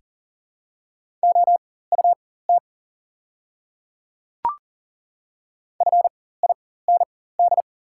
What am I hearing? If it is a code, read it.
Morse code, heard as a keyed tone, about 40 wpm, spaced as for 19 wpm, 690 Hz. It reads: OUT FIND
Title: 4x Character Spacing - Top 100 Words - 40wpm